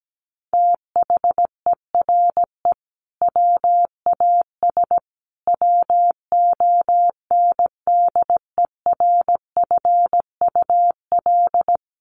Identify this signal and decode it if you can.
Morse code, heard as a keyed tone, about 17 wpm, 700 Hz. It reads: THERE WAS WONDERFUL
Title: Sentences from Top 1200 Words 17wpm